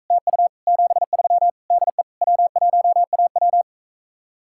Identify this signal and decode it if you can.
Morse code, heard as a keyed tone, about 45 wpm, 700 Hz. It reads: TU 73 DE W1AW